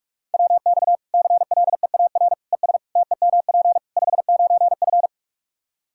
Morse code, heard as a keyed tone, about 46 wpm, 700 Hz. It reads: WX CLEAR ES TEMP 59F
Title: K1USN SST QSO Practice 45wpm